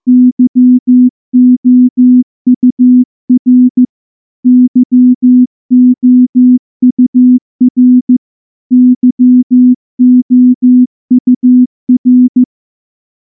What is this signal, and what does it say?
Morse code, heard as a keyed tone, about 14 wpm, 250 Hz. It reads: YOUR YOUR YOUR
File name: morse / morse-training-15WPM